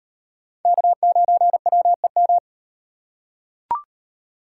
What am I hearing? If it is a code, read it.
Morse code, heard as a keyed tone, about 38 wpm, 700 Hz. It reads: K9WEM